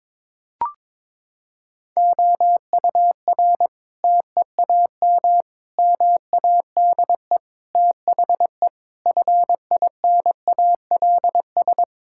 Morse code, heard as a keyed tone, about 22 wpm, 700 Hz. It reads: OUR TEAM MADE THE FINALS
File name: Sentences with 7-Word Phrases - 22wpm